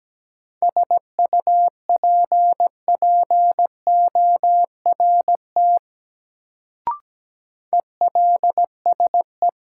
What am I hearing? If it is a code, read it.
Morse code, heard as a keyed tone, about 17 wpm, 700 Hz. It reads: SUPPORT ELSE
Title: Sentences from Top 800 Words - New Word Drill - 17wpm